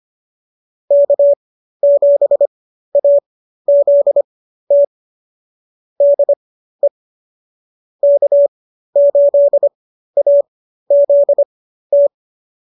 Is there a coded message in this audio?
Yes: Morse code from 0.9 to 12.1 s, heard as a keyed tone, about 25 wpm, spaced as for 13 wpm, 570 Hz. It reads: K7AZT DE K8AZT